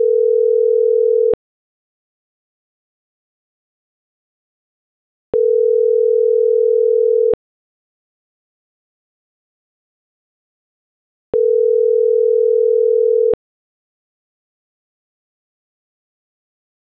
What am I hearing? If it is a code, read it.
Ringback tone (North America), 440 + 480 Hz.
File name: CENTRAL NMP / out marianacruznmp